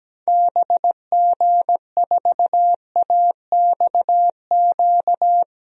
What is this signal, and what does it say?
Morse code, heard as a keyed tone, about 17 wpm, 700 Hz. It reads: BG4AXQ
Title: DXCC-100 Call Signs - 6 Characters - Repeated 2x - 17wpm